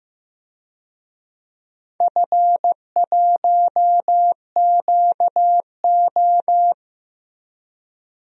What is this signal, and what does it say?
Morse code, heard as a keyed tone, about 15 wpm, 700 Hz. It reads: F1QO
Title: DXCC-100 Call Signs - 4 Characters - 15wpm